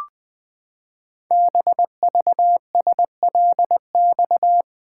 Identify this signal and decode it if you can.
Morse code, heard as a keyed tone, about 20 wpm, 700 Hz. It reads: BVSLX
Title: Code Groups - 5 Letter:Numbers - 20wpm